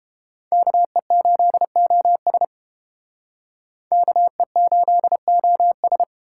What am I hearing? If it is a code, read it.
Morse code, heard as a keyed tone, about 31 wpm, 700 Hz. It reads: KE8OS KE8OS